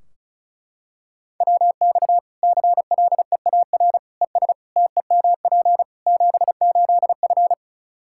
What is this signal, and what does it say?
Morse code, heard as a keyed tone, about 36 wpm, 700 Hz. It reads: WX CLEAR ES TEMP 78F
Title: K1USN SST QSO Practice 35wpm